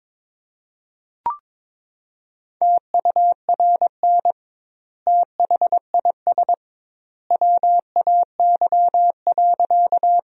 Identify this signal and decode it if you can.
Morse code, heard as a keyed tone, about 22 wpm, 700 Hz. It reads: TURN THIS WAY.
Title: 2x Word-Spacing - Sentences from Top 200 Words - 22wpm